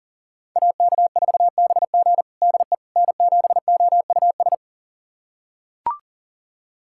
Morse code, heard as a keyed tone, about 37 wpm, 700 Hz. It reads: AK4BG DE N7OUS
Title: USA Callsign DE Callsign - 5 Characters - 40wpm